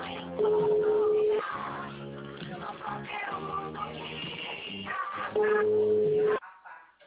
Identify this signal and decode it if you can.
Ringback tone (Europe, 425 Hz), with noise.